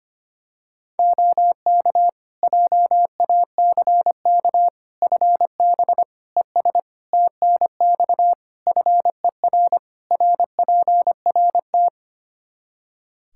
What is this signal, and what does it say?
Morse code, heard as a keyed tone, about 25 wpm, 700 Hz. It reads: OK JACK FB ES TNX FER RPRT
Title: Practice QSO - Exchange 2 - Info - 25wpm